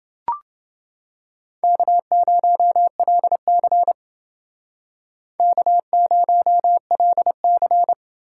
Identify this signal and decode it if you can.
Morse code, heard as a keyed tone, about 28 wpm, 700 Hz. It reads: K0LC K0LC